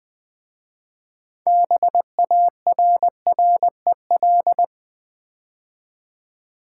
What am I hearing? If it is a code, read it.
Morse code, heard as a keyed tone, about 20 wpm, 700 Hz. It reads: BARREL
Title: 6 Letter Words - First 3 Letters Given - 20wpm